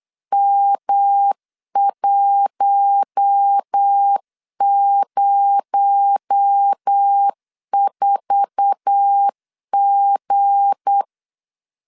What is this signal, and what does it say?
Morse code, heard as a keyed tone, about 8 wpm, 800 Hz. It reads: M104G
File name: 8wpm